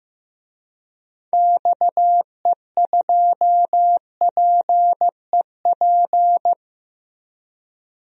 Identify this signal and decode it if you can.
Morse code, heard as a keyed tone, about 15 wpm, 700 Hz. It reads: XE2PEP
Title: DXCC-100 Call Signs - 6 Characters - Repeated 2x - 15wpm